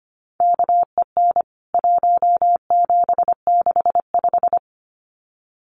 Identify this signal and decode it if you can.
Morse code, heard as a keyed tone, about 25 wpm, 700 Hz. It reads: KEN 1765
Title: CWT Exchange 25wpm